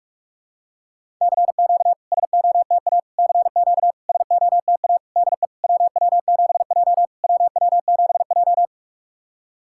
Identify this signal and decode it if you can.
Morse code, heard as a keyed tone, about 46 wpm, 690 Hz. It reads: CQ SOTA CQ SOTA DE WW7J WW7J
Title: SOTA QSO Practice 45wpm